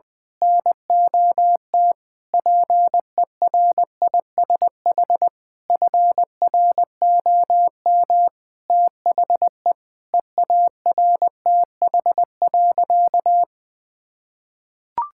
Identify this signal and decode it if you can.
Morse code, heard as a keyed tone, about 20 wpm, 700 Hz. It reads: NOT PERISH FROM THE EARTH.
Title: Sentences with 9-Word Phrases - 20wpm